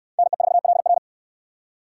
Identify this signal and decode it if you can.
Morse code, heard as a keyed tone, about 68 wpm, 690 Hz. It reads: NE4LR